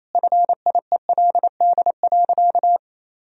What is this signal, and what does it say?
Morse code, heard as a keyed tone, about 28 wpm, 700 Hz. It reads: FIELD.
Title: Sentences from Top 1000 Words - Review Part 2 - 28wpm